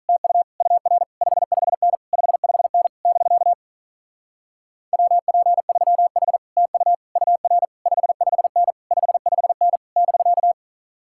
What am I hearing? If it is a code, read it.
Morse code, heard as a keyed tone, about 43 wpm, 700 Hz. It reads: TU UR 55N 55N <BK> WP3H TU UR 55N 55N <BK>
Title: Speed-Racing Common QSO Phrases Encore - 35wpm